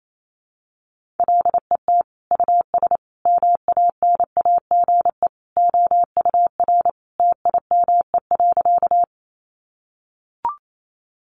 Morse code, heard as a keyed tone, about 28 wpm, 700 Hz. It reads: LET US MANAGE OUR TIME.
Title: Sentences from Top 1400 Words 28wpm